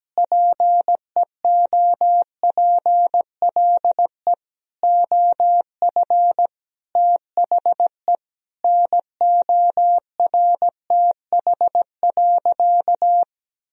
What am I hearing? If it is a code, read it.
Morse code, heard as a keyed tone, about 17 wpm, 700 Hz. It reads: PEOPLE OF THE NORTH.
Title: Sentences from Top 1000 Words - Review Part 1 - 17wpm